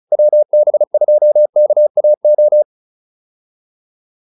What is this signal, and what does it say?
Morse code, heard as a keyed tone, about 34 wpm, 600 Hz. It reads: WB2KAO